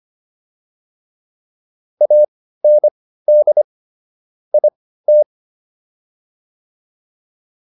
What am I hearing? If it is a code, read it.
Morse code, heard as a keyed tone, about 25 wpm, spaced as for 15 wpm, 600 Hz. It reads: AND IT